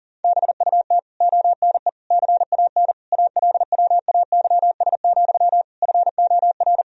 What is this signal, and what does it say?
Morse code, heard as a keyed tone, about 40 wpm, 700 Hz. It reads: BUT ONE CAN ALWAYS, FOR
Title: Sentences from Top 1000 Words - Review Part 4 - 40wpm